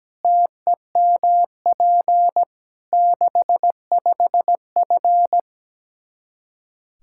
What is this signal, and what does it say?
Morse code, heard as a keyed tone, about 17 wpm, 700 Hz. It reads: TEMP 65F
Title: Practice QSO - Exchange 2 - Info - 17wpm